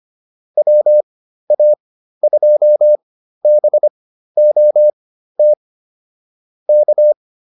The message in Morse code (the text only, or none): WA2BOT K